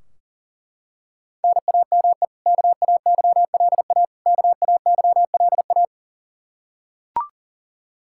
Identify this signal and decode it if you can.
Morse code, heard as a keyed tone, about 40 wpm, 700 Hz. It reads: NAME KAYLA KAYLA
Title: Practice QSO - Exchange 1 - Beginning - 40wpm